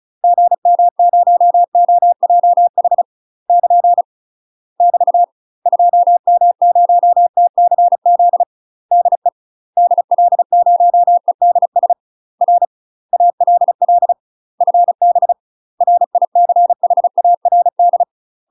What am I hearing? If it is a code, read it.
Morse code, heard as a keyed tone, about 35 wpm, 700 Hz. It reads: GM0OJH <KN> <BT> 2M0TCZ DE DL0EDS R ALL FB RICHARD
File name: UK MASTER QSOs TRAINING / 35WPM / UK MASTER QSOs 0042